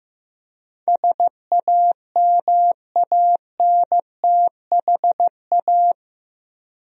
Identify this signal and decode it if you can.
Morse code, heard as a keyed tone, about 15 wpm, 700 Hz. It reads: SAMANTHA
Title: USA Common Names - 15wpm